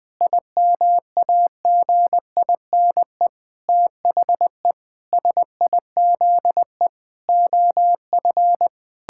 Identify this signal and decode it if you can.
Morse code, heard as a keyed tone, about 20 wpm, 700 Hz. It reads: IMAGINE THE SIZE OF
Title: Sentences with 7-Word Phrases - Spoken Beginning - 20wpm